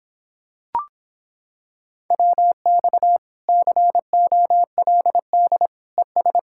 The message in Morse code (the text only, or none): WX COLD ES